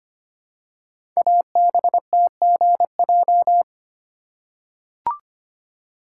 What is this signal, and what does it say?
Morse code, heard as a keyed tone, about 25 wpm, 700 Hz. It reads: ABTGJ
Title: Code Groups - 5 Letters - 25wpm